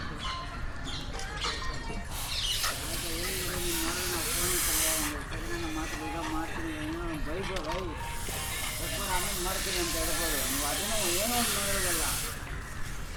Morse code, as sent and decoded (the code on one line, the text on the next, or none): none